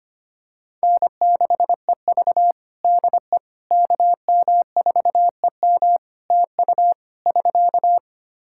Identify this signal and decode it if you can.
Morse code, heard as a keyed tone, about 25 wpm, 700 Hz. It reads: N6EV DE KM4EM TU <SK>
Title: Practice QSO - Exchange 4 - Ending - 25wpm